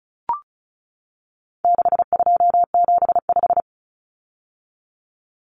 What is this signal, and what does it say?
Morse code, heard as a keyed tone, about 34 wpm, 700 Hz. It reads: 6275
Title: Common Cut Numbers - 4-Digit 35wpm